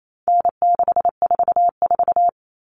Morse code, heard as a keyed tone, about 28 wpm, 700 Hz. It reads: N644
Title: Common Cut Numbers - 4-Digit 28wpm